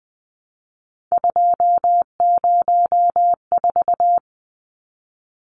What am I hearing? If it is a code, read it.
Morse code, heard as a keyed tone, about 20 wpm, 700 Hz. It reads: 204
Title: Numbers - 3-Digit 20wpm